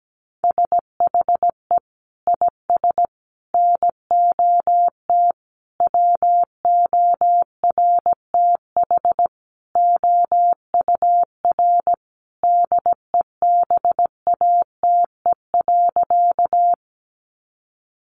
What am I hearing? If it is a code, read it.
Morse code, heard as a keyed tone, about 17 wpm, 700 Hz. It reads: SHE IS NOT WORTH OUR DEBATE.
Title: Sentences from Top 1100 Words 17wpm